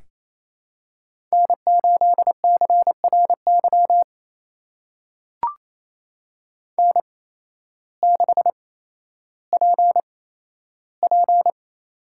Morse code, heard as a keyed tone, about 28 wpm, spaced as for 13 wpm, 700 Hz. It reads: N8CRY N6PP
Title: USA Call Signs - 8x Character Spacing - 28wpm